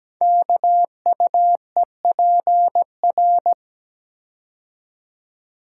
Morse code, heard as a keyed tone, about 17 wpm, 700 Hz. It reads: KUEPR
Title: Code Groups - 5 Letters - 17wpm